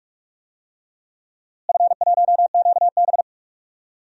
Morse code, heard as a keyed tone, about 44 wpm, 700 Hz. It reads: F1QB
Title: DXCC-100 Call Signs - 4 Characters - Repeated 2x - 45wpm